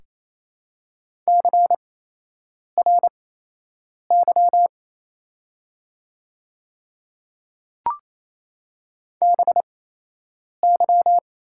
Morse code, heard as a keyed tone, about 28 wpm, spaced as for 8 wpm, 700 Hz. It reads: CRY BY
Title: 8x Character Spacing - Top 500 Words - 28wpm